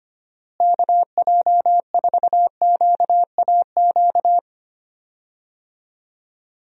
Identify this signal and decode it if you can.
Morse code, heard as a keyed tone, about 25 wpm, 700 Hz. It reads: KJ4QAQ
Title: USA Call Signs - 6 Characters - Repeated 2x - 25wpm